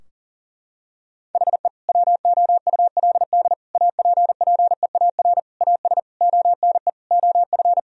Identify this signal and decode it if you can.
Morse code, heard as a keyed tone, about 40 wpm, 700 Hz. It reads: HE WOULD APPEAR AS ONE OF
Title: Sentences from Top 1000 Words - Review Part 2 - 40wpm